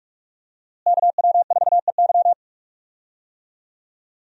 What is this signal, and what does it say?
Morse code, heard as a keyed tone, about 46 wpm, 700 Hz. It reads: KW4EY
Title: USA Call Signs - 5 Characters - Repeated 3x - 45wpm